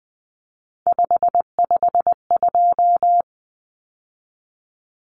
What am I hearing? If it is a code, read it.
Morse code, heard as a keyed tone, about 20 wpm, 700 Hz. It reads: 552
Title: Numbers - 3-Digit 20wpm